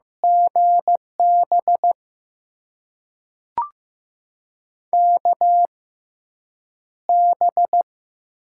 Morse code, heard as a keyed tone, about 15 wpm, spaced as for 8 wpm, 700 Hz. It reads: GB KB
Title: USA Call Signs - 6 Characters - 6x Character Spacing - 15wpm